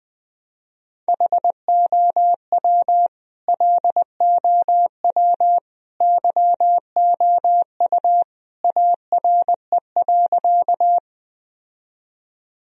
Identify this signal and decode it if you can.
Morse code, heard as a keyed tone, about 20 wpm, 700 Hz. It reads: HOW LOW YOU ARE.
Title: Sentences from Top 1000 Words - Review Part 1 - 20wpm